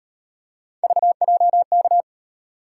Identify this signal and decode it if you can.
Morse code, heard as a keyed tone, about 36 wpm, 700 Hz. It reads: VJK